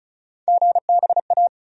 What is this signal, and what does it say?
Morse code, heard as a keyed tone, about 36 wpm, 700 Hz. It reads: GBA
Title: USA Call Sign Postfixes - 3 Letters 35wpm